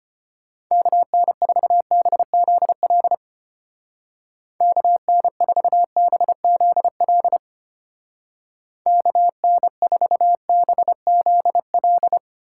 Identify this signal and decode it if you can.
Morse code, heard as a keyed tone, about 30 wpm, 700 Hz. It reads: KN4BZL KN4BZL KN4BZL
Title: Speed Racing - USA 2x3 Call Signs 25wpm